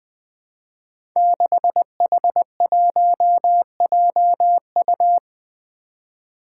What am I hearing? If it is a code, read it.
Morse code, heard as a keyed tone, about 20 wpm, 700 Hz. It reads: 6H1JU
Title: Five Challenging Letter-Number Combinations 20wpm